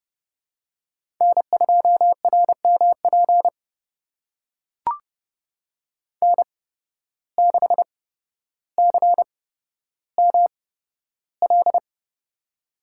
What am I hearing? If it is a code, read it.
Morse code, heard as a keyed tone, about 32 wpm, spaced as for 13 wpm, 700 Hz. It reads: N2RMP N6CML
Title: USA Call Signs - 8x Character Spacing - 30wpm